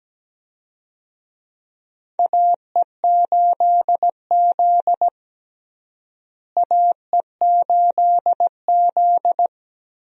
Morse code, heard as a keyed tone, about 17 wpm, 700 Hz. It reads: AE8Z AE8Z
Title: USA Call Signs - 4 Characters - Repeated 2x - 17wpm